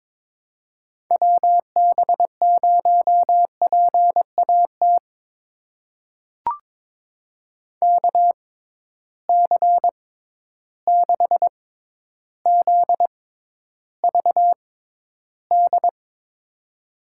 Morse code, heard as a keyed tone, about 22 wpm, spaced as for 11 wpm, 700 Hz. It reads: WB0PAT KC6ZVD
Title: USA Call Signs - 6 Characters - 6x Character Spacing - 22wpm